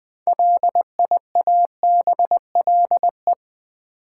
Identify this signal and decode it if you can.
Morse code, heard as a keyed tone, about 20 wpm, 700 Hz. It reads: LIABLE
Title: Build A Word - 2 Letters At A Time - 6-Letter Words - 20wpm